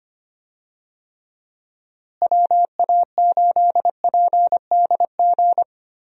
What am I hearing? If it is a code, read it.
Morse code, heard as a keyed tone, about 25 wpm, 700 Hz. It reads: WA8PDG
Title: USA Call Signs - 6 Characters - Repeated 3x - 25wpm